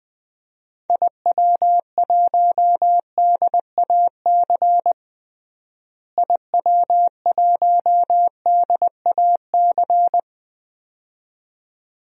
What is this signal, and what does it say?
Morse code, heard as a keyed tone, about 20 wpm, 700 Hz. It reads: IW1DAC IW1DAC
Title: DXCC-100 Call Signs - 6 Characters - Repeated 2x - 20wpm